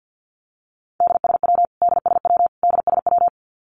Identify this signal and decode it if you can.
Morse code, heard as a keyed tone, about 49 wpm, 700 Hz. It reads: DSW DSW DSW
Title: Abbreviations Repeated 3x - 50wpm